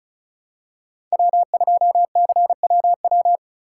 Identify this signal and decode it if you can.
Morse code, heard as a keyed tone, about 34 wpm, 700 Hz. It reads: W2CWW
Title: USA Call Signs - 6x Character Spacing - 35wpm